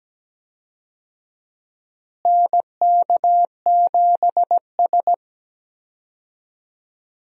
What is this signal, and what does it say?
Morse code, heard as a keyed tone, about 17 wpm, 700 Hz. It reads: NK7S
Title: USA Call Signs - 4 Characters - Repeated 3x - 17wpm